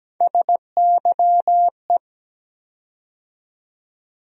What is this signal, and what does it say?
Morse code, heard as a keyed tone, about 17 wpm, 700 Hz. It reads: SYE
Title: USA Call Sign Postfixes - 3 Letters 17wpm